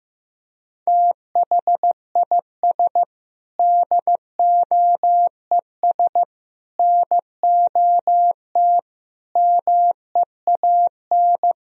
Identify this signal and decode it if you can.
Morse code, heard as a keyed tone, about 15 wpm, 700 Hz. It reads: THIS DOES NOT MEAN
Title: Sentences from Top 1000 Words - Review Part 4 - 15wpm